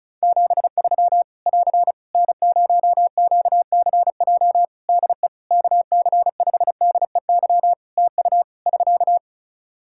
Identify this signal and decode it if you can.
Morse code, heard as a keyed tone, about 35 wpm, 700 Hz. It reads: 73 <AR> N0QCJ DE KC5DEY TU <SK>
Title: Practice QSO - Exchange 4 - Ending - 35wpm